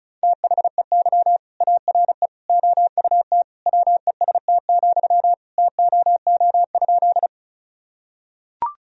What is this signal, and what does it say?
Morse code, heard as a keyed tone, about 36 wpm, 700 Hz. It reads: THEY ARE OUT WEST, TOO?
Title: Sentences from Top 1000 Words - Review Part 2 - 35wpm